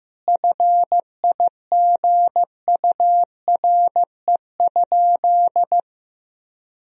Morse code, heard as a keyed tone, about 15 wpm, 700 Hz. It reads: FIGURE?
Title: Sentences from Top 1000 Words - Review Part 2 - 15wpm